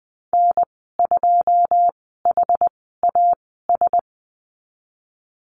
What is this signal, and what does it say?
Morse code, heard as a keyed tone, about 20 wpm, spaced as for 15 wpm, 700 Hz. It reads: N2HAS
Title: 173 Lesson 28g - Number 9 - Call Signs 20wpm